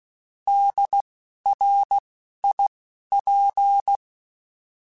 Morse code, heard as a keyed tone, about 16 wpm, 800 Hz. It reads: D R I P
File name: Alphabetical / Alphabetical mp3s / Ds @10